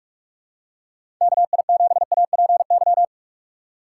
Morse code, heard as a keyed tone, about 46 wpm, 700 Hz. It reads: KI7APY
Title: USA Call Signs - 6x Character Spacing - 45wpm